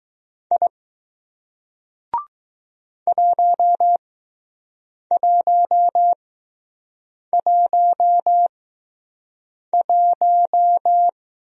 Morse code, heard as a keyed tone, about 20 wpm, spaced as for 6 wpm, 700 Hz. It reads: I 1111